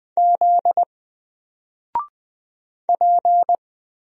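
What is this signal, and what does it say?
Morse code, heard as a keyed tone, about 20 wpm, 700 Hz. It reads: Z P
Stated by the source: Single letter and number Morse drill at 20 wpm, from N4LCD.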